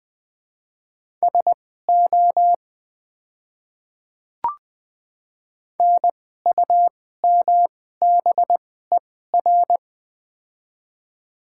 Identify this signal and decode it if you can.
Morse code, heard as a keyed tone, about 21 wpm, spaced as for 15 wpm, 700 Hz. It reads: SO NUMBER
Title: 2x Character Spacing - Top 100 Words - 20wpm